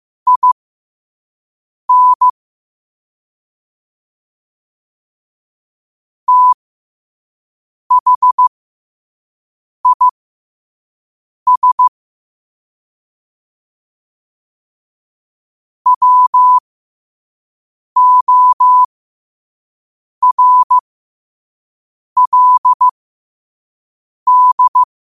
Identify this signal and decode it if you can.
Morse code, heard as a keyed tone, about 15 wpm, spaced as for 5 wpm, 1000 Hz. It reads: IN THIS WORLD
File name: qotdfiles / 5wpm